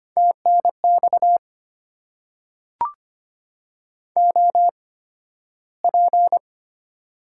Morse code, heard as a keyed tone, about 25 wpm, spaced as for 13 wpm, 700 Hz. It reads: TNX OP
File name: Top 90 QSO Elements - 8x Character Spacing - 25wpm